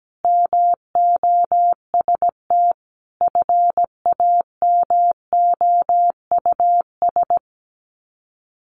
Morse code, heard as a keyed tone, about 17 wpm, 700 Hz. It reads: MOST FAMOUS
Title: Sets of 5 Words Encore2 17wpm